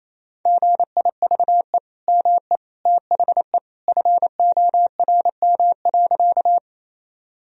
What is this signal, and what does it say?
Morse code, heard as a keyed tone, about 28 wpm, 700 Hz. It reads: GIVE ME THE FORM.
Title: Sentences from Top 1000 Words - Review Part 1 - 28wpm